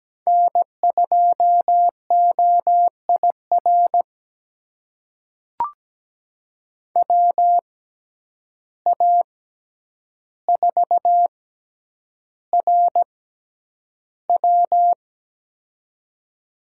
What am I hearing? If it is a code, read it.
Morse code, heard as a keyed tone, about 17 wpm, spaced as for 9 wpm, 700 Hz. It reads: N2OIR WA4RW
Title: USA Call Signs - 5 Characters - 6x Character Spacing - 17wpm